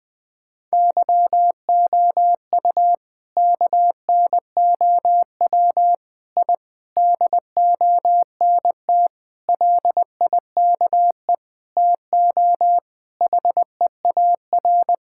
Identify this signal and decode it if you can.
Morse code, heard as a keyed tone, about 20 wpm, 700 Hz. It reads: YOU KNOW I DONT LIKE TO HEAR
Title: Sentences from Top 1000 Words - Review Part 6 - 20wpm